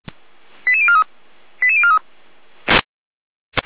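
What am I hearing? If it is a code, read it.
Five-tone selective call (ZVEI), heard as quick runs of five tones, tones of about 70 ms. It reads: 88043 88043
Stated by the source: ZVEI five-tone selective call 88074 received with an SDR.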